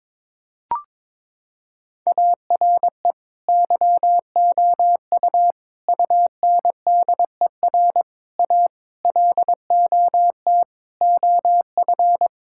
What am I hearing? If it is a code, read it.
Morse code, heard as a keyed tone, about 22 wpm, 700 Hz. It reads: ARE YOU UNDER A LOT OF
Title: Sentences from Top 1000 Words - Review Part 6 - 22wpm